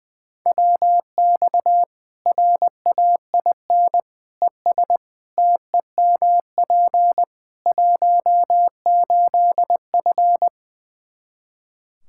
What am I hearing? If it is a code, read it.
Morse code, heard as a keyed tone, about 20 wpm, 700 Hz. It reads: WX RAIN ES TEMP 18F
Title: K1USN SST QSO Practice 20wpm